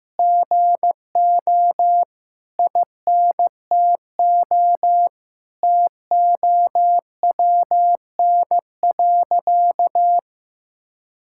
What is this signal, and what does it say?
Morse code, heard as a keyed tone, about 15 wpm, 700 Hz. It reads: GO INTO TOWN.
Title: Sentences from Top 1000 Words - Review Part 6 - 15wpm